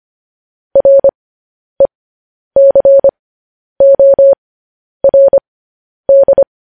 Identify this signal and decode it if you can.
Morse code, heard as a keyed tone, about 25 wpm, spaced as for 10 wpm, 550 Hz. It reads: RECORD
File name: podcast / News headlines-1x-25x10